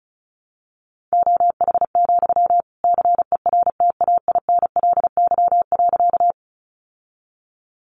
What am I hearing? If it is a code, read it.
Morse code, heard as a keyed tone, about 34 wpm, 700 Hz. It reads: OH, CERTAINLY.
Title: Sentences from Top 900 Words 35wpm